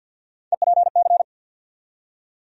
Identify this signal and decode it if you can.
Morse code, heard as a keyed tone, about 49 wpm, 700 Hz. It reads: EPC
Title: USA Call Sign Postfixes - 3 Letters 50wpm